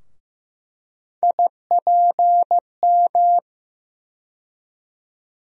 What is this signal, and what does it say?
Morse code, heard as a keyed tone, about 15 wpm, 700 Hz. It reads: IPM
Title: USA Call Sign Postfixes - 3 Letters 15wpm